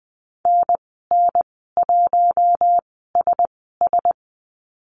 Morse code, heard as a keyed tone, about 20 wpm, spaced as for 14 wpm, 700 Hz. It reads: NN1SS
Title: 042 Lesson 8g - Number 1 - Call Signs 20wpm